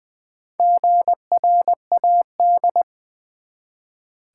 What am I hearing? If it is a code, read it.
Morse code, heard as a keyed tone, about 20 wpm, 700 Hz. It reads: GRAD
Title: Build A Word - 1 Letter At A Time - 5-Letter Words - 20wpm